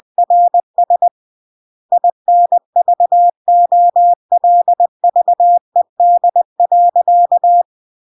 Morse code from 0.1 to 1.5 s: RS